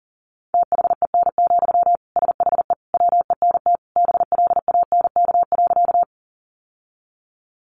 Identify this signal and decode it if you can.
Morse code, heard as a keyed tone, about 40 wpm, 700 Hz. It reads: THEN, SHE WENT BLANK.
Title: Sentences from Top 1500 Words 40wpm